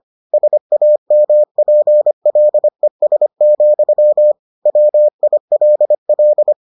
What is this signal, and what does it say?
Morse code, heard as a keyed tone, about 25 wpm, 600 Hz. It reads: SAMPLES, WILL